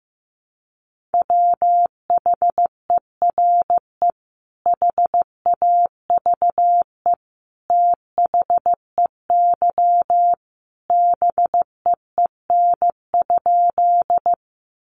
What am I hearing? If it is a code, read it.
Morse code, heard as a keyed tone, about 15 wpm, 700 Hz. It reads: WHERE HAVE THEY BEEN?
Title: Sentences from Top 200 Words 15wpm